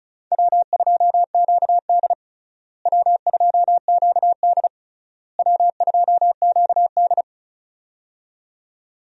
Morse code, heard as a keyed tone, about 36 wpm, 700 Hz. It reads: W2QD W2QD W2QD
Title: USA Call Signs - 4 Characters - Repeated 3x - 35wpm